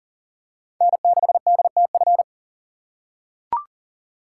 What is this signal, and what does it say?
Morse code, heard as a keyed tone, about 40 wpm, 700 Hz. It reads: N6DTF
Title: USA Call Signs - 5 Characters - 3x Character Spacing - 40wpm